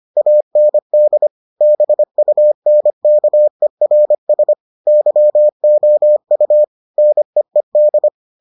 Morse code, heard as a keyed tone, about 25 wpm, 600 Hz. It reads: AND BUNKERS YOU NEED